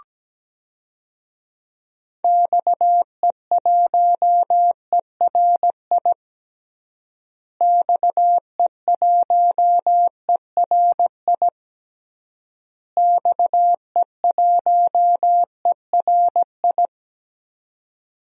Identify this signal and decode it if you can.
Morse code, heard as a keyed tone, about 17 wpm, 700 Hz. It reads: XE1ERI XE1ERI XE1ERI
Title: DXCC-100 Call Signs - 6 Characters - Repeated 3x - 17wpm